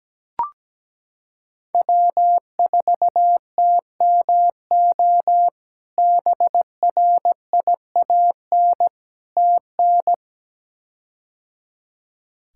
Morse code, heard as a keyed tone, about 17 wpm, 700 Hz. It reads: W4TMO BRIAN TN
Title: K1USN SST QSO Practice - 17wpm